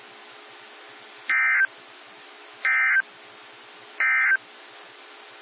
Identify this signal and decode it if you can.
Emergency Alert System (SAME) tones, heard as harsh buzzing bursts. It is heard from 1.2 to 4.5 s.
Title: NOAA Weather Radio - WXJ73